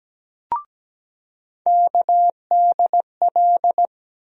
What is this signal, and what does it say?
Morse code, heard as a keyed tone, about 17 wpm, 700 Hz. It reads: KDL
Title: USA Call Sign Postfixes - 3 Letters 17wpm